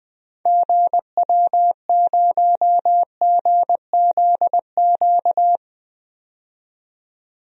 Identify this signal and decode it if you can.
Morse code, heard as a keyed tone, about 21 wpm, 700 Hz. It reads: GW0GZQ
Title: DXCC-100 Call Signs - 6 Characters - 20wpm